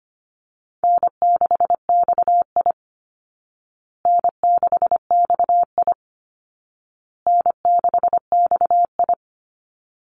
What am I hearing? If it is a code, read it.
Morse code, heard as a keyed tone, about 25 wpm, 700 Hz. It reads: N6XI N6XI N6XI
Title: Easy Call Signs Repeated 3x - 25wpm